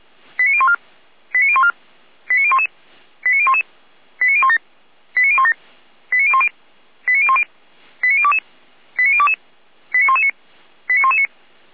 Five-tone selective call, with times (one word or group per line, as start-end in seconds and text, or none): 0.4-0.7 89014
1.3-1.7 89014
2.3-2.7 89011
3.2-3.6 89011
4.2-4.6 89017
5.2-5.5 89017
6.1-6.5 89010
7.1-7.4 89010
8.0-8.4 89022
9.0-9.3 89022
9.9-10.3 89119
10.9-11.3 89119